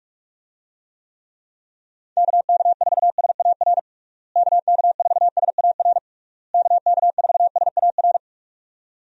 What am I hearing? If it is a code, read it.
Morse code, heard as a keyed tone, about 46 wpm, 700 Hz. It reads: KK4SAR KK4SAR KK4SAR
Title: USA Call Signs - 6 Characters - Repeated 3x - 45wpm